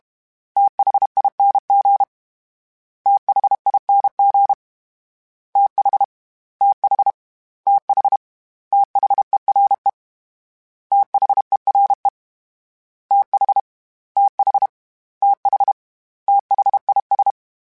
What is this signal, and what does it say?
Morse code, heard as a keyed tone, about 32 wpm, 800 Hz. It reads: THING THING TH TH TH THERE THERE TH TH TH THIS
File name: TH / TH mp3s / TH Prefixes 3x2 @30